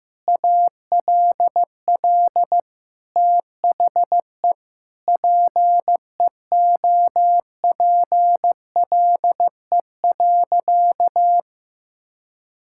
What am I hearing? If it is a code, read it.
Morse code, heard as a keyed tone, about 15 wpm, 700 Hz. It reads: ALL THE PEOPLE.
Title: Sentences from Top 1600 Words 15wpm